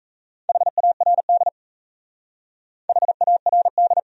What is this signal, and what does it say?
Morse code, heard as a keyed tone, about 35 wpm, 700 Hz. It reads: HARD HARD
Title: Speed-Racing Top 500 Words 28wpm